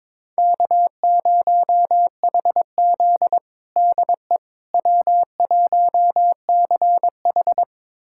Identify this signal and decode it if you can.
Morse code, heard as a keyed tone, about 22 wpm, 700 Hz. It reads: K0HZ DE W1CH
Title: USA Callsign DE Callsign - 4 Characters - 22wpm